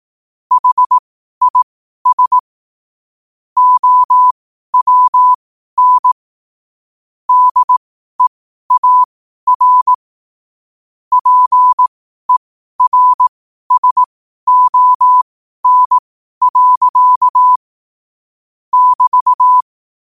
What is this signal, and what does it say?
Morse code, heard as a keyed tone, about 18 wpm, spaced as for 12 wpm, 1000 Hz. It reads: HIS OWN DEAR PERSON. <BT>
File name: 12wpm